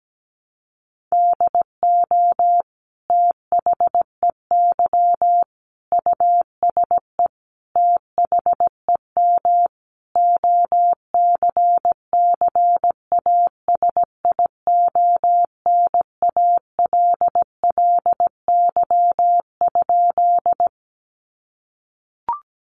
Morse code, heard as a keyed tone, about 17 wpm, 700 Hz. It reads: DO THEY USE THEM OCCASIONALLY?
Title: Sentences from Top 1300 Words 17wpm